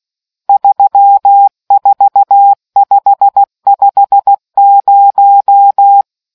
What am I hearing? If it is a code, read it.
Morse code, heard as a keyed tone, about 16 wpm, 800 Hz. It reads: 34550